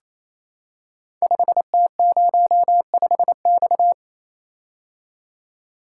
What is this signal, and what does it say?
Morse code, heard as a keyed tone, about 28 wpm, 700 Hz. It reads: 5T05X
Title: Code Groups - 5 Letter:Numbers - 28wpm